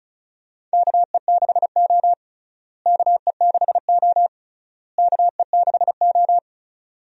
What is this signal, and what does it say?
Morse code, heard as a keyed tone, about 36 wpm, 700 Hz. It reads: KE6O KE6O KE6O
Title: USA Call Signs - 4 Characters - Repeated 3x - 35wpm